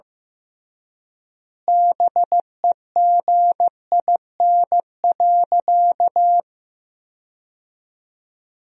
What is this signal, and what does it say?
Morse code, heard as a keyed tone, about 15 wpm, 700 Hz. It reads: BEGIN.
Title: 3x Word-Spacing - Sentences from Top 300 Words - 15wpm